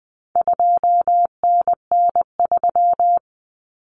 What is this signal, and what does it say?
Morse code, heard as a keyed tone, about 20 wpm, 700 Hz. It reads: 2NN3
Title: Common Cut Numbers - 4-Digit 20wpm